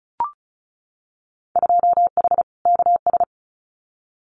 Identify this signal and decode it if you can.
Morse code, heard as a keyed tone, about 34 wpm, 700 Hz. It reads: 2H KS
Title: Winter Field Day Exchanges Encore 35wpm